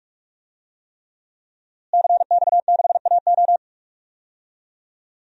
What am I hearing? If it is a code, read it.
Morse code, heard as a keyed tone, about 46 wpm, 700 Hz. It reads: CX6AO
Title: DXCC-100 Call Signs - 5 Characters - Repeated 3x - 45wpm